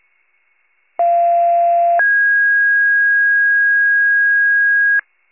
Two-tone page, heard as a one-second tone, then a three-second tone, about 690 then 1650 Hz. It is heard from 1.0 to 5.0 s.
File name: wcalert